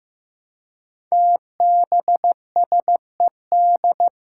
Morse code, heard as a keyed tone, about 15 wpm, 700 Hz. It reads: TBSED
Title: Five Challenging Letter-Number Combinations 15wpm